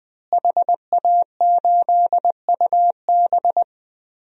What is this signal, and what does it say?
Morse code, heard as a keyed tone, about 20 wpm, 700 Hz. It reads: HA8UB
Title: DXCC-100 Call Signs - 5 Characters - Repeated 3x - 20wpm